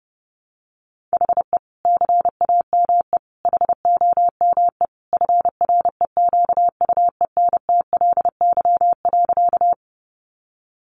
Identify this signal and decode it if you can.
Morse code, heard as a keyed tone, about 30 wpm, 700 Hz. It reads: HE CAME HOME FREQUENTLY.
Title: Sentences from Top 1300 Words 30wpm